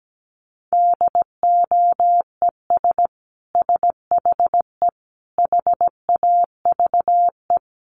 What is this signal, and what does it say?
Morse code, heard as a keyed tone, about 17 wpm, 700 Hz. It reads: DOES SHE HAVE
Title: Sentences from Top 800 Words 17wpm